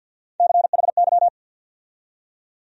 Morse code, heard as a keyed tone, about 46 wpm, 700 Hz. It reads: CHY
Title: USA Call Sign Postfixes - 3 Letters 50wpm